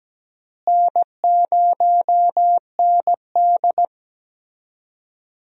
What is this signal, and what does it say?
Morse code, heard as a keyed tone, about 17 wpm, 700 Hz. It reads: N0ND
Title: USA Call Signs - 4 Characters - Repeated 3x - 17wpm